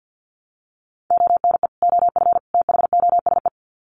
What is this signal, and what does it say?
Morse code, heard as a keyed tone, about 48 wpm, 700 Hz. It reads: ONE OF THOSE
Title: Sets of 3 Words 50wpm